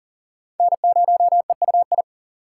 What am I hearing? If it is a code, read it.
Morse code, heard as a keyed tone, about 40 wpm, 700 Hz. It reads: N0EUI
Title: USA Call Signs - 3x Character Spacing - 40wpm